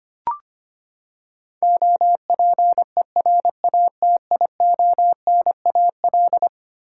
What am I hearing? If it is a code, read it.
Morse code, heard as a keyed tone, about 25 wpm, 700 Hz. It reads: OPERATIONAL
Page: morse-code-ninja-practice-25wpm